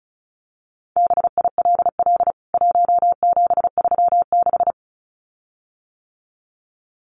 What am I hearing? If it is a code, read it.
Morse code, heard as a keyed tone, about 34 wpm, 700 Hz. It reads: BILL 1736
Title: CWT Exchange 35wpm